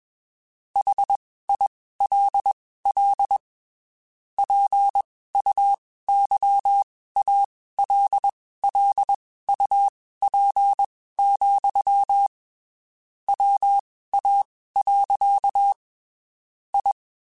Morse code, heard as a keyed tone, about 21 wpm, spaced as for 15 wpm, 800 Hz. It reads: HILL PUYALLUP, WA. I
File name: (spaced) / QSOs Spaced-mp3s / WA7QOX 2 @20 sp